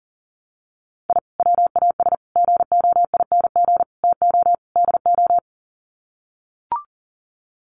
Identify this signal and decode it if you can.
Morse code, heard as a keyed tone, about 37 wpm, 700 Hz. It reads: I WAS GOING TO DO
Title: Sets of 5 Words Encore2 40wpm